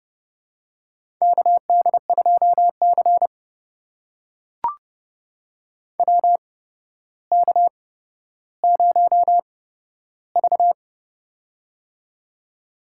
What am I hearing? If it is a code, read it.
Morse code, heard as a keyed tone, about 30 wpm, spaced as for 13 wpm, 700 Hz. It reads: KD2C WK0V
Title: USA Call Signs - 4 Characters - 8x Character Spacing - 30wpm